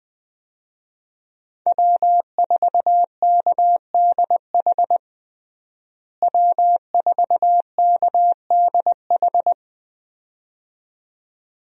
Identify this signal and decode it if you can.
Morse code, heard as a keyed tone, about 21 wpm, 700 Hz. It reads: W4KDH W4KDH
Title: USA Call Signs - 5 Characters - Repeated 2x - 20wpm